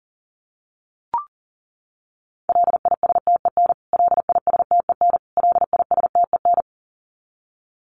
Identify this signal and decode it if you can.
Morse code, heard as a keyed tone, about 40 wpm, 700 Hz. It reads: LISTEN LISTEN LISTEN
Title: Top 500 Words Repeated 40wpm